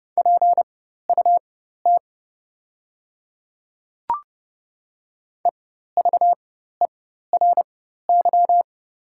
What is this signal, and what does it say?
Morse code, heard as a keyed tone, about 30 wpm, spaced as for 14 wpm, 700 Hz. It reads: PUT EVERY
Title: 4x Character Spacing - Top 200 Words - 30wpm